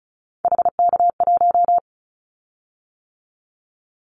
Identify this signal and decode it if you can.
Morse code, heard as a keyed tone, about 36 wpm, 700 Hz. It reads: HK1
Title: Callsign Prefixes for Top 100 DXCC - 35wpm